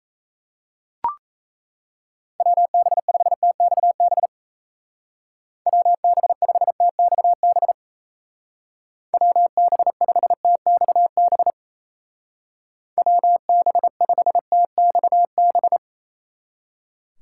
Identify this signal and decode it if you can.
Morse code, heard as a keyed tone, about 36 wpm, 700 Hz. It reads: WB5TXB WB5TXB WB5TXB WB5TXB